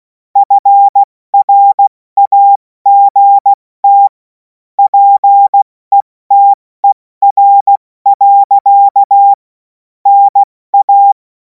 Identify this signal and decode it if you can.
Morse code, heard as a keyed tone, about 16 wpm, 800 Hz. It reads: FRAGT PETER. NA
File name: Morsekurs / Witze / Witz 07. April 2026 70 F80